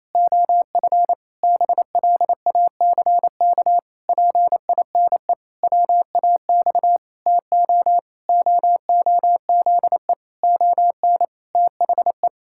Morse code, heard as a keyed tone, about 28 wpm, 700 Hz. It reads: OF BLACK PINE WAX TO OOZE ON THE
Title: Pangram Challenge v2 - 28wpm